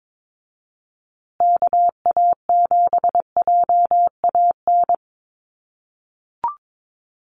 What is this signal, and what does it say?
Morse code, heard as a keyed tone, about 22 wpm, 700 Hz. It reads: KA7JAN
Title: USA Call Signs Repeated 2x 22wpm